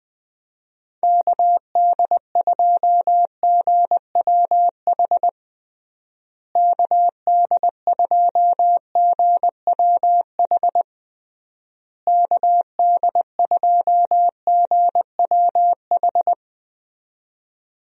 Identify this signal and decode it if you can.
Morse code, heard as a keyed tone, about 20 wpm, 700 Hz. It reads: KD2GWH KD2GWH KD2GWH
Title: USA Call Signs - 6 Characters - Repeated 3x - 20wpm